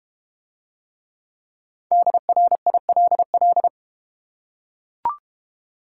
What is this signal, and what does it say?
Morse code, heard as a keyed tone, about 31 wpm, 700 Hz. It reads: DRILL
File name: Easy Words - Speed-Racing with Hints - 25wpm